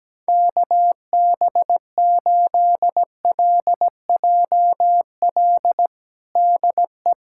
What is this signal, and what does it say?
Morse code, heard as a keyed tone, about 17 wpm, 700 Hz. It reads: KB8LJL DE
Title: USA Callsign DE Callsign 17wpm